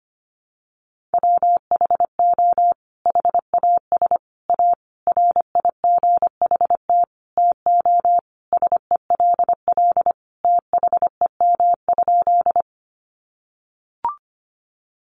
Morse code, heard as a keyed tone, about 25 wpm, 700 Hz. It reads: WHO HAS A RIGHT TO SELL THEM?
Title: Sentences from Top 1100 Words 25wpm